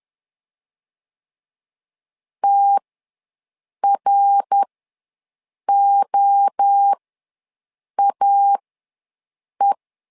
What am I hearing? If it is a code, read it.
Morse code, heard as a keyed tone, about 10 wpm, spaced as for 6 wpm, 800 Hz. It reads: TROAE
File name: LEZIONI / 3 MWRO / 3 MWRO ws 10wpm 2310